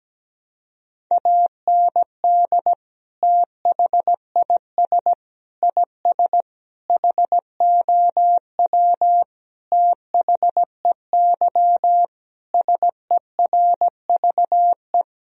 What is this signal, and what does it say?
Morse code, heard as a keyed tone, about 17 wpm, 700 Hz. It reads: AND THIS IS HOW THEY SERVE
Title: Sentences from Top 1000 Words - Review Part 2 - 17wpm